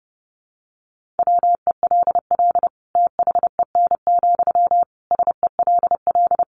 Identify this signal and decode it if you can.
Morse code, heard as a keyed tone, about 30 wpm, 700 Hz. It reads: WELL THEN, SELL
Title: Sentences from Top 1100 Words 30wpm